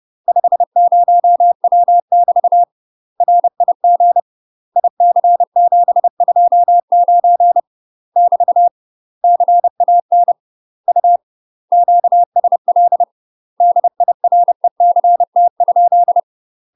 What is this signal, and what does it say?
Morse code, heard as a keyed tone, about 30 wpm, 700 Hz. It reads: 50W <BT> RIG IC729 <BT> CAN U QSL DIRECT?
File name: UK MASTER QSOs TRAINING / 30WPM / UK MASTER QSOs 0011